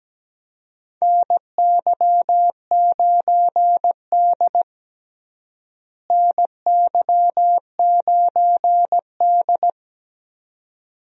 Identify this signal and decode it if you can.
Morse code, heard as a keyed tone, about 17 wpm, 700 Hz. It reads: NY9D NY9D
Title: USA Call Signs - 4 Characters - Repeated 2x - 17wpm